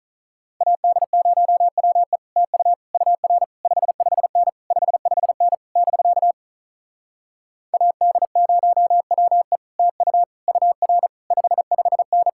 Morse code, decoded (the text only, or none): AD0WE TU UR 55N 55N <BK> AD0WE TU UR 55N